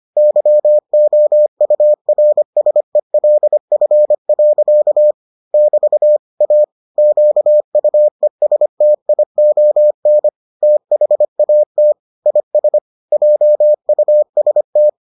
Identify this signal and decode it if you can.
Morse code, heard as a keyed tone, about 25 wpm, 600 Hz. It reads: YOURSELF. <BT> A QUESTION THAT IS JUST